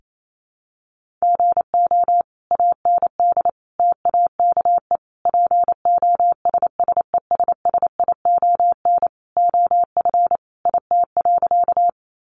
Morse code, heard as a keyed tone, about 28 wpm, 700 Hz. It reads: GO AND TAKE POSSESSION OF IT.
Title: Sentences from Top 1400 Words 28wpm